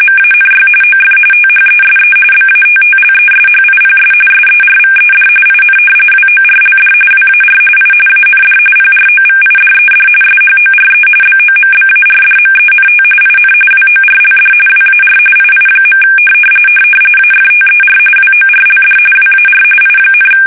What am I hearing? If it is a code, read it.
RTTY, about 45.45 baud. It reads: MANY OF THESE BATTLESHIPS SERVED IN WORLD WAR I AND PROVIDED SHORE BOMBARDMENT IN WORLD WAR II. AFTER WORLD WAR I,
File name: fsk850 / battleshipnj